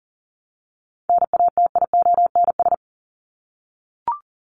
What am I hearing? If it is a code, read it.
Morse code, heard as a keyed tone, about 39 wpm, 700 Hz. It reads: NATIONS
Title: ES:S Suffix - Top 100 Words with Base Words 40wpm